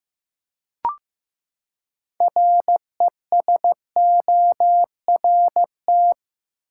Morse code, heard as a keyed tone, about 15 wpm, 700 Hz. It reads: RESORT
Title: Six Letter Words - 15wpm